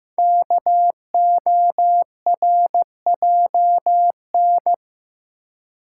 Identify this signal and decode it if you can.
Morse code, heard as a keyed tone, about 15 wpm, 700 Hz. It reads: KORJN
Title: Code Groups - 5 Letters - 15wpm